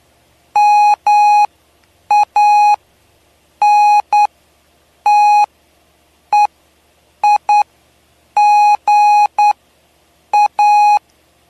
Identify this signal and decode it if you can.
Morse code, heard as a keyed tone, about 9 wpm, spaced as for 7 wpm, 830 Hz. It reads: MANTEIGA
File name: cw / py2lcd / 07-PY2LCD-Li��o 7